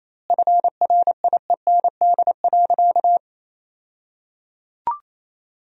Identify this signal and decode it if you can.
Morse code, heard as a keyed tone, about 28 wpm, 700 Hz. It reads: FRIEND.
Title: 2x Word-Spacing - Sentences from Top 300 Words - 28wpm